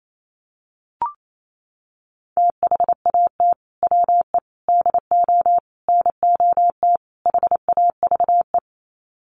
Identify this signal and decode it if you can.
Morse code, heard as a keyed tone, about 28 wpm, 700 Hz. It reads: THAT WE DO NOT HAVE
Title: Sets of 5 Words Encore 28wpm